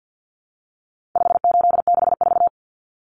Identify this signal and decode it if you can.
Morse code, heard as a keyed tone, about 50 wpm, 700 Hz. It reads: 5864
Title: Numbers - 4-Digit 50wpm